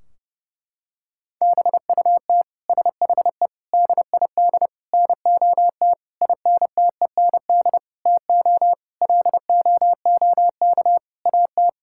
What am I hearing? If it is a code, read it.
Morse code, heard as a keyed tone, about 32 wpm, 700 Hz. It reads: BUT SHE DID NOT INTEND TO LOOK AT
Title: Sentences from Top 1600 Words 30wpm